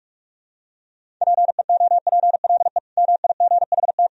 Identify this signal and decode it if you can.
Morse code, heard as a keyed tone, about 41 wpm, 690 Hz. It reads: PEOPLE MIGHT
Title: Sentences from Top 500 Words and Abbreviations 45wpm